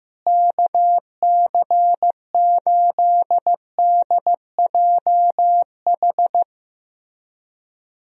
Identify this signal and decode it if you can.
Morse code, heard as a keyed tone, about 15 wpm, 700 Hz. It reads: KC8DJH
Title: USA Call Signs - 6 Characters - Repeated 3x - 15wpm